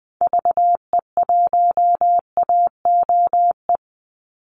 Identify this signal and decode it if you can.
Morse code, heard as a keyed tone, about 20 wpm, 700 Hz. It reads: VE1AOE
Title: Call Signs - 20wpm